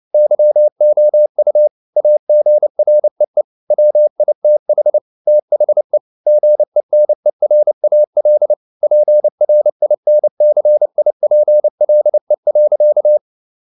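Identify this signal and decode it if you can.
Morse code, heard as a keyed tone, about 29 wpm, 600 Hz. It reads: YOU AGREE WITH THE GENERAL PRINCIPLE.
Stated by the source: Morse code at 29 wpm, sending Ken Follett's novel Never in English, from CW med Gnister.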